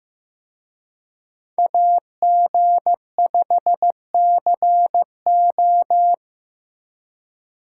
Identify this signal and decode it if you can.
Morse code, heard as a keyed tone, about 18 wpm, 700 Hz. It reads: AG5CO
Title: Speed Racing - USA 2x2 Call Signs 15wpm